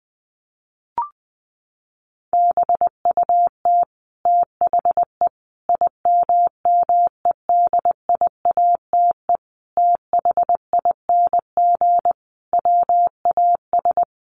Morse code, heard as a keyed tone, about 19 wpm, 700 Hz. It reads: BUT THE IMMEDIATE THING WAS
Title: A Sound Of Thunder 20wpm